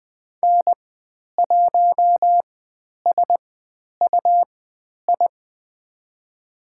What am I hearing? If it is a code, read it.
Morse code, heard as a keyed tone, about 20 wpm, spaced as for 10 wpm, 700 Hz. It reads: N1SUI
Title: 109 Lesson 18g - Letter m - Call Signs 20wpm